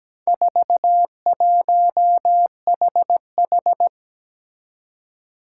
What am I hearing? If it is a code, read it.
Morse code, heard as a keyed tone, about 17 wpm, 700 Hz. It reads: 41HH
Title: Four Challenging Letter-Number Combinations 17wpm